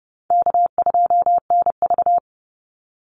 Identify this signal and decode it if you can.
Morse code, heard as a keyed tone, about 29 wpm, 700 Hz. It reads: K2NV
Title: Easy Call Signs Repeated 2x - 30wpm